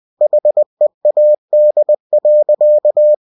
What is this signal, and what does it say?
Morse code, heard as a keyed tone, about 20 wpm, 600 Hz. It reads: HEAD.
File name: cw-books / winnie-the-pooh / winnie-20wpm-page-0030